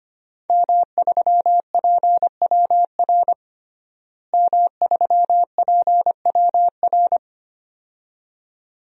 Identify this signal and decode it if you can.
Morse code, heard as a keyed tone, about 25 wpm, 700 Hz. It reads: M3PWR M3PWR
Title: DXCC-100 Call Signs - 5 Characters - Repeated 2x - 25wpm